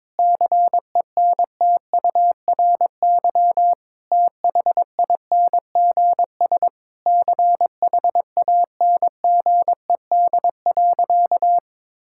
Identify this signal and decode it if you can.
Morse code, heard as a keyed tone, about 22 wpm, 700 Hz. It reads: CENTURY THINGS CHANGED.
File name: practice / 22wpm / Sentences with 4-Word Phrases - 22wpm